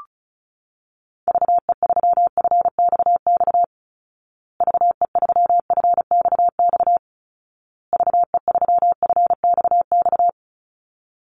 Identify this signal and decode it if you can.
Morse code, heard as a keyed tone, about 34 wpm, 700 Hz. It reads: VE3FXX VE3FXX VE3FXX
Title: Call Signs Repeated 2x - 35wpm